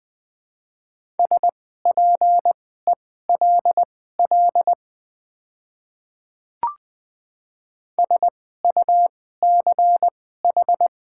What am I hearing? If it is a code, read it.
Morse code, heard as a keyed tone, about 20 wpm, spaced as for 15 wpm, 700 Hz. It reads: SPELL SUCH
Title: 2x Character Spacing - Top 200 Words - 20wpm